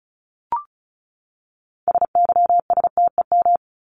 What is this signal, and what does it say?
Morse code, heard as a keyed tone, about 34 wpm, 700 Hz. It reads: SYSTEM
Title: Top 501-750 Words 35wpm